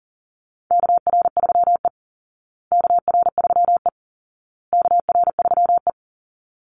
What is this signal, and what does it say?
Morse code, heard as a keyed tone, about 40 wpm, 700 Hz. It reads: KR3E KR3E KR3E
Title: Call Signs Repeated 2x - 40wpm